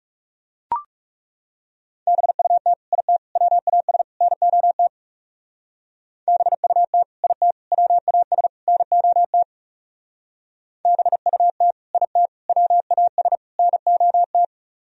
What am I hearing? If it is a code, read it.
Morse code, heard as a keyed tone, about 38 wpm, 700 Hz. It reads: BUT IT WAS NOT BUT IT WAS NOT BUT IT WAS NOT